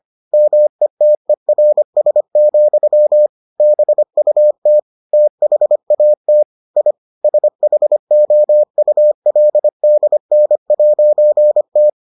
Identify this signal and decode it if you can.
Morse code, heard as a keyed tone, about 25 wpm, 600 Hz. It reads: METERS, BUT THAT I SHOULDN'T